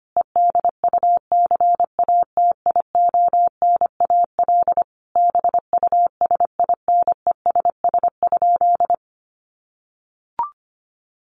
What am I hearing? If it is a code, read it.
Morse code, heard as a keyed tone, about 25 wpm, 700 Hz. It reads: EDUCATIONAL BUSINESS?
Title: Sentences from Top 1200 Words 25wpm